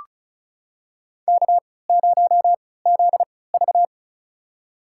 Morse code, heard as a keyed tone, about 36 wpm, spaced as for 20 wpm, 700 Hz. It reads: K0ZV
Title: USA Call Signs - 4 Characters - 3x Character Spacing - 35wpm